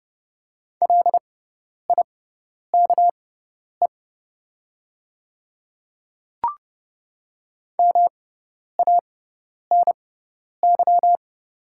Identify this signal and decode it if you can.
Morse code, heard as a keyed tone, about 30 wpm, spaced as for 10 wpm, 700 Hz. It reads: LIKE MANY
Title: 6x Character Spacing - Top 100 Words - 30wpm